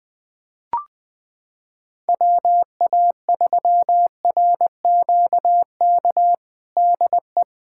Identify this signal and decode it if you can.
Morse code, heard as a keyed tone, about 20 wpm, 700 Hz. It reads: WA3RQK DE
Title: USA Callsign DE Callsign - 6 Characters - 20wpm